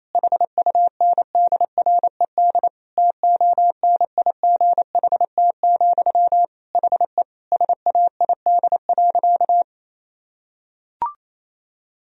Morse code, heard as a keyed tone, about 27 wpm, 700 Hz. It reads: HUNDRED TONIGHT, HE SAID.
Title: Sentences from Top 1000 Words - Review Part 2 - 28wpm